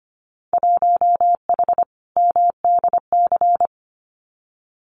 Morse code, heard as a keyed tone, about 25 wpm, 700 Hz. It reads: 1H MDC
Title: Winter Field Day Exchanges 25wpm